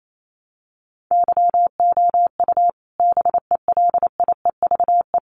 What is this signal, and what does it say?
Morse code, heard as a keyed tone, about 28 wpm, 700 Hz. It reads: YOU BELIEVE
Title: Sentences from Top 1000 Words 28wpm